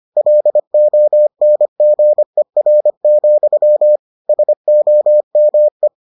Morse code, heard as a keyed tone, about 25 wpm, 600 Hz. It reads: LONGER, SOME